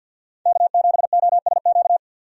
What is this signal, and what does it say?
Morse code, heard as a keyed tone, about 52 wpm, 700 Hz. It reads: K7OSQ
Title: USA Call Signs - 5 Characters - 50wpm